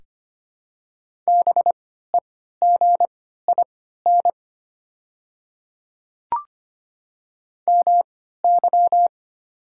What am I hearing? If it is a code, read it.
Morse code, heard as a keyed tone, about 25 wpm, spaced as for 14 wpm, 690 Hz. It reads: BEGIN MY
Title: 3x Character Spacing - Top 500 Words - 25wpm